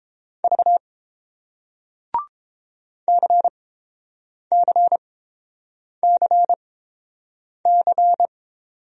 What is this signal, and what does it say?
Morse code, heard as a keyed tone, about 30 wpm, spaced as for 7 wpm, 700 Hz. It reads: V CCCC